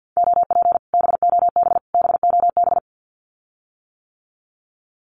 Morse code, heard as a keyed tone, about 48 wpm, 700 Hz. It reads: OP BOB BOB
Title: Common QSO Phrases Encore 50wpm